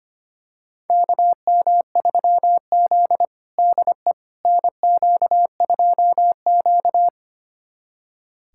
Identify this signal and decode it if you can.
Morse code, heard as a keyed tone, about 25 wpm, 700 Hz. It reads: KM3Z DE NQ2Q
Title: USA Callsign DE Callsign - 4 Characters - 25wpm